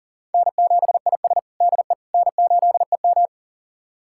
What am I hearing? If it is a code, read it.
Morse code, heard as a keyed tone, about 40 wpm, 700 Hz. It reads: N7IS DE N8EM
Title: USA Callsign DE Callsign - 4 Characters - 40wpm